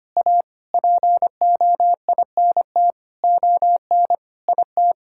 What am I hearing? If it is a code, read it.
Morse code, heard as a keyed tone, about 25 wpm, 700 Hz. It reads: A POINT ON IT
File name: practice / 25wpm / Sentences with 9-Word Phrases - 25wpm